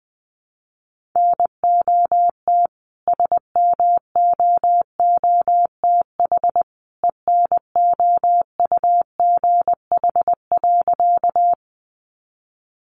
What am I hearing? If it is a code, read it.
Morse code, heard as a keyed tone, about 20 wpm, 700 Hz. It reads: NOT SMOOTH ENOUGH.
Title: Sentences from Top 1400 Words 20wpm